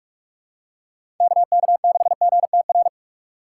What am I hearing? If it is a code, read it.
Morse code, heard as a keyed tone, about 42 wpm, 700 Hz. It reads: KK6GTR